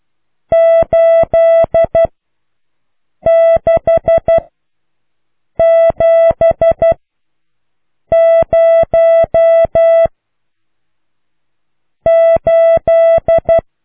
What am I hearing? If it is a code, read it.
Morse code, heard as a keyed tone, about 12 wpm, 660 Hz. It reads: 8 6 7 0 8